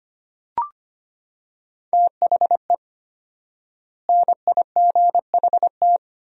Morse code, heard as a keyed tone, about 25 wpm, 700 Hz. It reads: THE NIGHT
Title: 4x Word-Spacing - Sentences from Top 500 Words - 25wpm